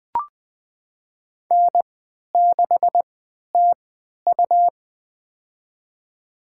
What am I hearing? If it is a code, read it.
Morse code, heard as a keyed tone, about 20 wpm, spaced as for 11 wpm, 700 Hz. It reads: N6TU
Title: USA Call Signs - 3x Character Spacing - 20wpm